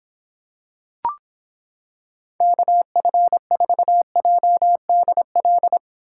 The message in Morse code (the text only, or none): KF4JDL